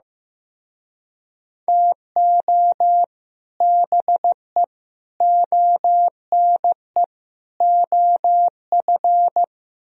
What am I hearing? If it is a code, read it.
Morse code, heard as a keyed tone, about 17 wpm, 700 Hz. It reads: TO BE ONE OF
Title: Speed-Racing - Sets of 4 Words 15wpm